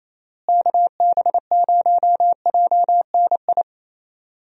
Morse code, heard as a keyed tone, about 28 wpm, 700 Hz. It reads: KB0JNI
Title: USA Call Signs - 6 Characters - Repeated 2x - 28wpm